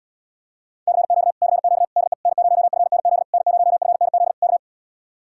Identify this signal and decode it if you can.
Morse code, heard as a keyed tone, about 77 wpm, 700 Hz. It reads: CQ CQ DE N0FNZ N0FNZ K